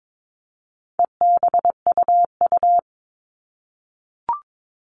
Morse code, heard as a keyed tone, about 22 wpm, 700 Hz. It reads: EBUU
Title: All Cut Numbers - 4-Digit 22wpm